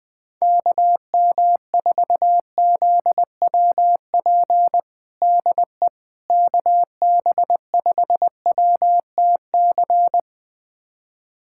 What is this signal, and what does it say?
Morse code, heard as a keyed tone, about 21 wpm, 700 Hz. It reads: KM4ZWP DE KB5WTC
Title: USA Callsign DE Callsign 20wpm